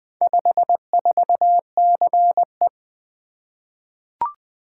Morse code, heard as a keyed tone, about 20 wpm, 700 Hz. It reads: 54CE
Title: Four Letter-Number Combinations - 20wpm